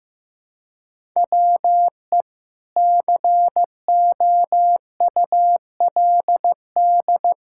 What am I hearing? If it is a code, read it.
Morse code, heard as a keyed tone, about 15 wpm, 700 Hz. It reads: WE COULD
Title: Pangram Challenge v2 - 15wpm